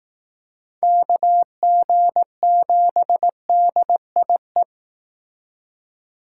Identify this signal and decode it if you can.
Morse code, heard as a keyed tone, about 19 wpm, 700 Hz. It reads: KG7DIE